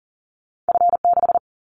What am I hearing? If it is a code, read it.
Morse code, heard as a keyed tone, about 38 wpm, 700 Hz. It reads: F6
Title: Callsign Prefixes for Top 25 DXCC - 40wpm